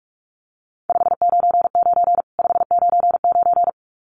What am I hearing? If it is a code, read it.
Morse code, heard as a keyed tone, about 44 wpm, 700 Hz. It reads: 599 599
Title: Common QSO Phrases 45wpm